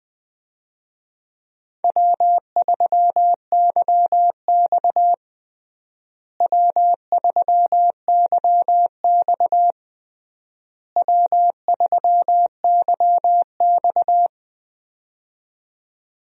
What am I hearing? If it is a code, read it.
Morse code, heard as a keyed tone, about 20 wpm, 700 Hz. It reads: W3YX W3YX W3YX
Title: USA Call Signs - 4 Characters - Repeated 3x - 20wpm